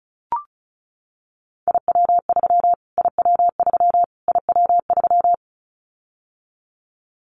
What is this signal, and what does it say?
Morse code, heard as a keyed tone, about 34 wpm, 700 Hz. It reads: IW3 IW3 IW3
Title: Callsign Prefixes for Top 25 DXCC - Repeated 3x - 35wpm